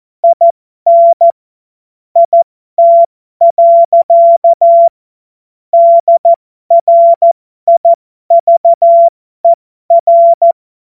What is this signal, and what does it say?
Morse code, heard as a keyed tone, about 14 wpm, spaced as for 12 wpm, 680 Hz. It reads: IN IT. DRIVER